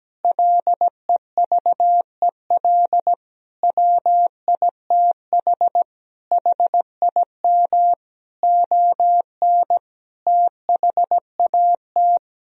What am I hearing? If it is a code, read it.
Morse code, heard as a keyed tone, about 17 wpm, 700 Hz. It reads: LEVEL WITH HIM ON THAT
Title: Sentences from Top 1000 Words - Review Part 4 - 17wpm